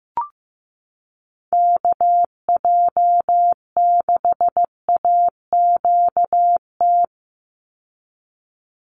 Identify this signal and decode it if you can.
Morse code, heard as a keyed tone, about 15 wpm, 700 Hz. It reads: KJ6AQT
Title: USA Call Signs 15wpm